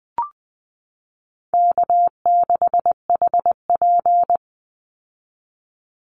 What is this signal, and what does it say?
Morse code, heard as a keyed tone, about 20 wpm, 700 Hz. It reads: K6HP
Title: Easy Call Signs - 20wpm